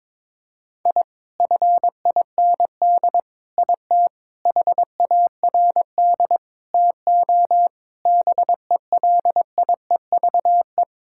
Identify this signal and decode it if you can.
Morse code, heard as a keyed tone, about 22 wpm, 700 Hz. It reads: I FIND IT HARD TO BELIEVE
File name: practice / 22wpm / Sets of 7 Words - 22wpm